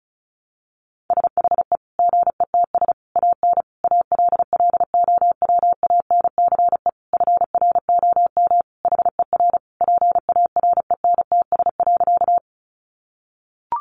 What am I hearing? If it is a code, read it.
Morse code, heard as a keyed tone, about 34 wpm, 700 Hz. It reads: SHE GETS AN ALLOWANCE FROM HER PARENTS.
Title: Easy Phrases 35wpm